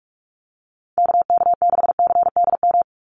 Morse code, heard as a keyed tone, about 44 wpm, 700 Hz. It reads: KK6CDM
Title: USA Call Signs 45wpm